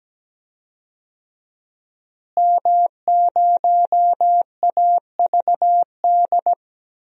Morse code, heard as a keyed tone, about 17 wpm, 700 Hz. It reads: M0AVD
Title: DXCC-100 Call Signs - 5 Characters - Repeated 2x - 17wpm